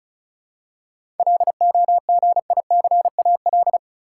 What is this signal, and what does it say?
Morse code, heard as a keyed tone, about 34 wpm, 700 Hz. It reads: LOGICAL
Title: Seven Letter Words - 35wpm